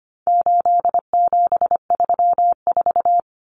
Morse code, heard as a keyed tone, about 25 wpm, 700 Hz. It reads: 8734
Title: Common Cut Numbers - 4-Digit 25wpm